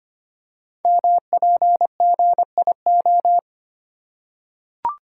Morse code, heard as a keyed tone, about 25 wpm, 700 Hz. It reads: MPGIO
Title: Code Groups - 5 Letters - 25wpm